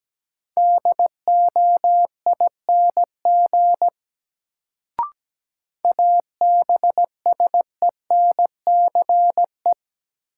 Morse code, heard as a keyed tone, about 17 wpm, 700 Hz. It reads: DOING ABSENCE
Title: Sets of 4 Words - New Word Drill - 17wpm